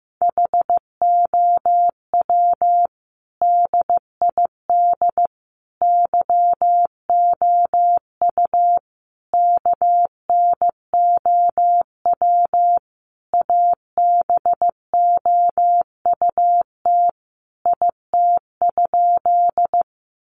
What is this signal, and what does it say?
Morse code, heard as a keyed tone, about 15 wpm, 700 Hz. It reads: HOW DID YOU KNOW ABOUT IT?
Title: Sentences from Top 100 Words 15wpm